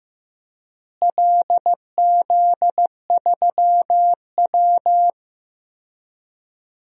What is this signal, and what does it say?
Morse code, heard as a keyed tone, about 15 wpm, 700 Hz. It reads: LZ3W
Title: DXCC-100 Call Signs - 4 Characters - Repeated 3x - 15wpm